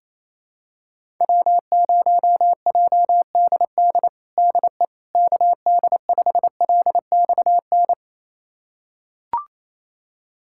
Morse code, heard as a keyed tone, about 28 wpm, 700 Hz. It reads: W0JDD DE KD5LXN
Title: USA Callsign DE Callsign 28wpm